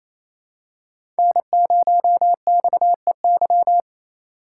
Morse code, heard as a keyed tone, about 28 wpm, 700 Hz. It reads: N0XEY
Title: USA Call Signs - 5 Characters - 2x Character Spacing - 28wpm